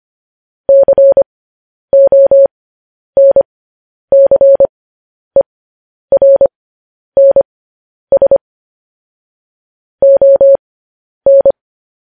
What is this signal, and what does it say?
Morse code, heard as a keyed tone, about 25 wpm, spaced as for 10 wpm, 550 Hz. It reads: CONCERNS ON
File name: podcast / News headlines-1x-25x10